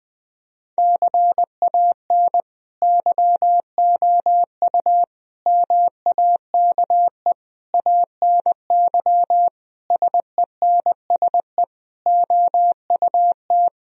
Morse code, heard as a keyed tone, about 20 wpm, 700 Hz. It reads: CAN YOU MAKE ANY SENSE OUT
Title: Sentences from Top 1000 Words - Review Part 4 - 20wpm